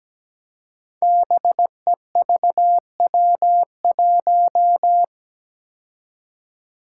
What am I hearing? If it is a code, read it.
Morse code, heard as a keyed tone, about 17 wpm, 700 Hz. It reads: BEVW1
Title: Five Challenging Letter-Number Combinations 17wpm